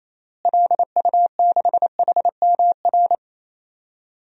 Morse code, heard as a keyed tone, about 28 wpm, 700 Hz. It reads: LU6HMR
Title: DXCC-100 Call Signs - 6 Characters - Repeated 3x - 28wpm